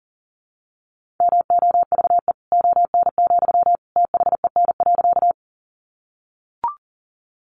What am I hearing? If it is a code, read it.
Morse code, heard as a keyed tone, about 40 wpm, 700 Hz. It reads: MOVE ON, THEN.
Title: Sentences from Top 200 Words 40wpm